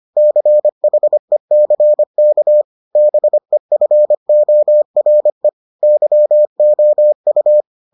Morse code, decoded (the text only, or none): CHECK BEFORE YOU